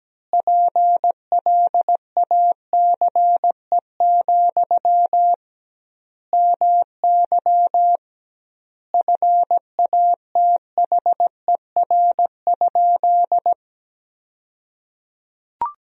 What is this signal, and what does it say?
Morse code, heard as a keyed tone, about 17 wpm, 700 Hz. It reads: PLACE, MY FATHER?
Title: 2x Word-Spacing - Sentences from Top 400 Words - 17wpm